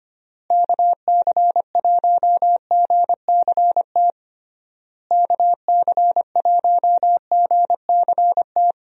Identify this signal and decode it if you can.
Morse code, heard as a keyed tone, about 25 wpm, 700 Hz. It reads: KC1GCT KC1GCT
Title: USA Call Signs - 6 Characters - Repeated 3x - 25wpm